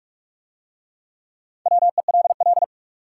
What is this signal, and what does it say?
Morse code, heard as a keyed tone, about 40 wpm, 700 Hz. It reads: WELL